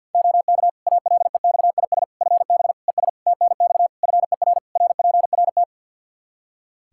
Morse code, heard as a keyed tone, about 48 wpm, 700 Hz. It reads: OK ALEXIS FB ES TNX FER RPRT
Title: Practice QSO - Exchange 2 - Info - 50wpm